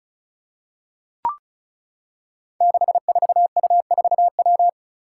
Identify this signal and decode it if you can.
Morse code, heard as a keyed tone, about 36 wpm, 700 Hz. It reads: 64U4W
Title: Five Challenging Letter-Number Combinations 35wpm